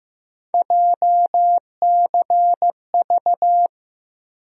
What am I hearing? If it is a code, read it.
Morse code, heard as a keyed tone, about 15 wpm, 700 Hz. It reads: JCV
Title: USA Call Sign Postfixes - 3 Letters 15wpm